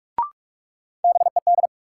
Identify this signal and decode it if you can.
Morse code, heard as a keyed tone, about 46 wpm, 700 Hz. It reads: BED
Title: Sentences from Top 500 Words - New Word Drill - 45wpm